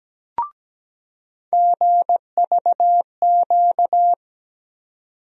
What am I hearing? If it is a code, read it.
Morse code, heard as a keyed tone, about 17 wpm, 700 Hz. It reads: GVQ
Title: USA Call Sign Postfixes - 3 Letters 17wpm